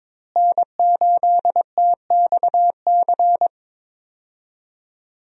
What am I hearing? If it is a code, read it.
Morse code, heard as a keyed tone, about 21 wpm, 700 Hz. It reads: N8TXC
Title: USA Call Signs - 5 Characters - Repeated 2x - 22wpm